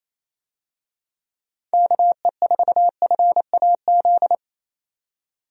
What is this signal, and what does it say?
Morse code, heard as a keyed tone, about 27 wpm, 700 Hz. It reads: KE4FAZ
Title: USA Call Signs - 6 Characters - 28wpm